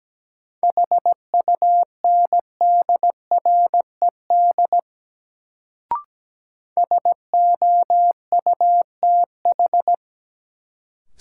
Morse code, heard as a keyed tone, about 17 wpm, 700 Hz. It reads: HUNDRED SOUTH
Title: Sentences from Top 400 Words - New Word Drill - 17wpm